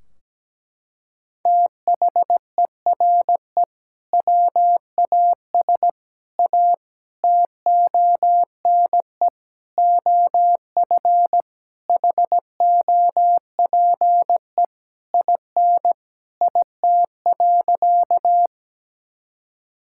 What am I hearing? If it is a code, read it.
Morse code, heard as a keyed tone, about 17 wpm, 700 Hz. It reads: THERE WAS A TONE OF HOPE IN IT.
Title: Sentences from Top 1600 Words 17wpm